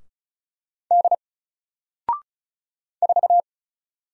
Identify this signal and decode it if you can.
Morse code, heard as a keyed tone, about 35 wpm, 700 Hz. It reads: D 4
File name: Single-Letter-Number-35wpm